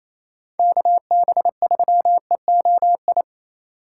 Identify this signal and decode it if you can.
Morse code, heard as a keyed tone, about 28 wpm, 700 Hz. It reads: KB3EOI
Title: USA Call Signs - 6 Characters - 2x Character Spacing - 28wpm